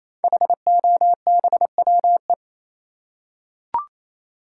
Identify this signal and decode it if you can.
Morse code, heard as a keyed tone, about 28 wpm, 700 Hz. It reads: HOBWE
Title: Code Groups - 5 Letters - 28wpm